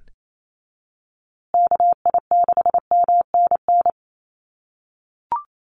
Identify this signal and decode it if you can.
Morse code, heard as a keyed tone, about 28 wpm, 700 Hz. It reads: KI6MNN
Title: USA Call Signs Repeated 2x 28wpm